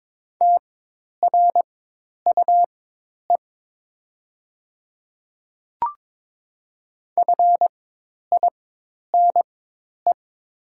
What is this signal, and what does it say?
Morse code, heard as a keyed tone, about 22 wpm, spaced as for 10 wpm, 700 Hz. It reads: TRUE FINE
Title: 4x Character Spacing - Top 500 Words - 22wpm